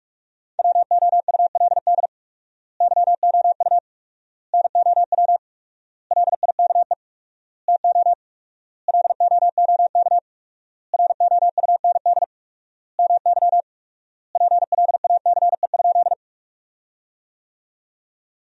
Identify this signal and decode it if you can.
Morse code, heard as a keyed tone, about 41 wpm, 700 Hz. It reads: WOULD YOU NOW LIKE TO LOOK ROUND MY PLACE?
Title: 4x Word-Spacing - Sentences from Top 200 Words - 45wpm